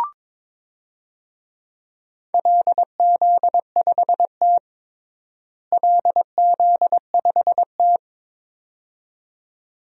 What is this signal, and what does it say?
Morse code, heard as a keyed tone, about 22 wpm, 700 Hz. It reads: LZ5T LZ5T
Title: DXCC-100 Call Signs - 4 Characters - Repeated 2x - 22wpm